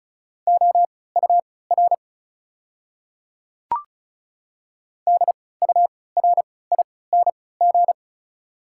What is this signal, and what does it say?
Morse code, heard as a keyed tone, about 36 wpm, spaced as for 20 wpm, 690 Hz. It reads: OUR DURING
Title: 3x Character Spacing - Top 500 Words - 35wpm